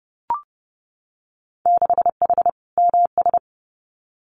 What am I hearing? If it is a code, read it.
Morse code, heard as a keyed tone, about 30 wpm, 700 Hz. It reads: 6H MS
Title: Winter Field Day Exchanges Encore 30wpm